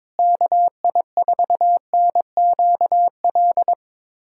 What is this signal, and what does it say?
Morse code, heard as a keyed tone, about 22 wpm, 700 Hz. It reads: KI4NQL
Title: USA Callsign DE Callsign 22wpm